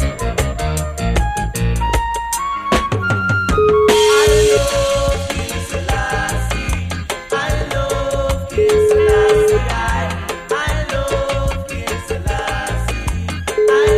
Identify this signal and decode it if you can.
Ringback tone (Europe, 425 Hz), with noise.